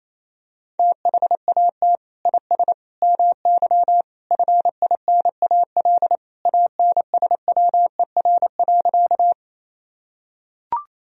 Morse code, heard as a keyed tone, about 28 wpm, 700 Hz. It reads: THAT IS MY FINAL ANSWER.
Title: Sentences from Top 1000 Words - Review Part 2 - 28wpm